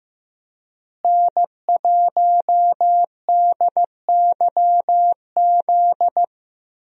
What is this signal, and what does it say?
Morse code, heard as a keyed tone, about 16 wpm, 700 Hz. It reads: N1DYZ
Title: USA Call Signs - 5 Characters - Repeated 3x - 15wpm